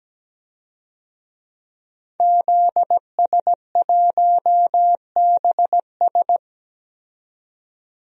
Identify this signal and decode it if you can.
Morse code, heard as a keyed tone, about 17 wpm, 700 Hz. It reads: ZS1BS
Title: DXCC-100 Call Signs - 5 Characters - 17wpm